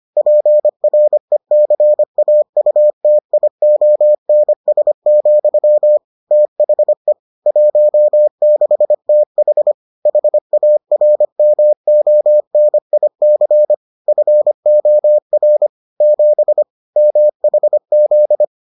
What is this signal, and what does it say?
Morse code, heard as a keyed tone, about 25 wpm, 600 Hz. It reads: PRECAUTIONS, THE 16TH HARMONIC FOR 7 MHZ